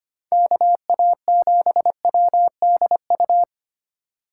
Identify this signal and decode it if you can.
Morse code, heard as a keyed tone, about 25 wpm, 700 Hz. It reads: KA7WDU
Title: USA Call Signs - 6 Characters - 4x Character Spacing - 25wpm